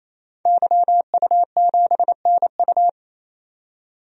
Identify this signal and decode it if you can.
Morse code, heard as a keyed tone, about 28 wpm, 700 Hz. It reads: YU7NU
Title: DXCC-100 Call Signs - 5 Characters - 28wpm